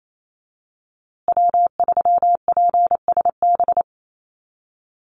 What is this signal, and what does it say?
Morse code, heard as a keyed tone, about 28 wpm, 700 Hz. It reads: W3PSB
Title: USA Call Signs Repeated 2x 28wpm